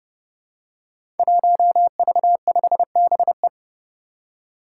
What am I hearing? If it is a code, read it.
Morse code, heard as a keyed tone, about 31 wpm, 700 Hz. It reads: 1V5BE
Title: Five Challenging Letter-Number Combinations 30wpm